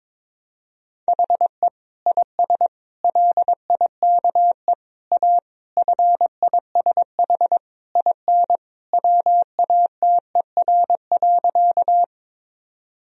Morse code, heard as a keyed tone, about 22 wpm, 700 Hz. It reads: HE IS LIKE A FISH IN WATER.
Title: Sentences from Top 1000 Words - Review Part 1 - 22wpm